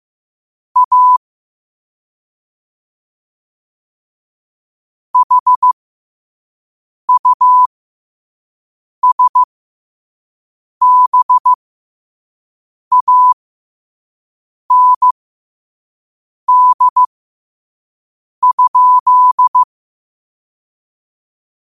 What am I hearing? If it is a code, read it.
Morse code, heard as a keyed tone, about 15 wpm, spaced as for 5 wpm, 1000 Hz. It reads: A HUSBAND?
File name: qotdfiles / 5wpm